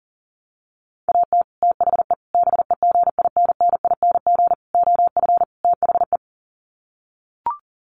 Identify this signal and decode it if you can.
Morse code, heard as a keyed tone, about 40 wpm, 700 Hz. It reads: AT THE BEGINNING OF THE
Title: Sets of 5 Words 40wpm